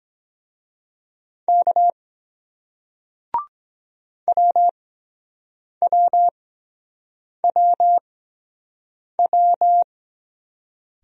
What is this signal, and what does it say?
Morse code, heard as a keyed tone, about 23 wpm, spaced as for 7 wpm, 700 Hz. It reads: K WWWW